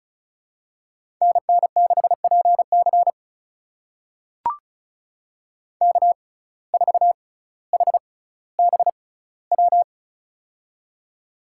Morse code, heard as a keyed tone, about 36 wpm, spaced as for 18 wpm, 700 Hz. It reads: NN6PC K4HBW
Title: USA Call Signs - 5 Characters - 6x Character Spacing - 35wpm